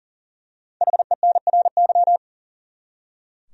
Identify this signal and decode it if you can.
Morse code, heard as a keyed tone, about 40 wpm, 700 Hz. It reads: HENRY
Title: USA Common Names - Max 5 Letters - 40wpm